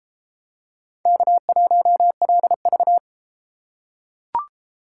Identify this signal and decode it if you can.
Morse code, heard as a keyed tone, about 32 wpm, 700 Hz. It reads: K1LV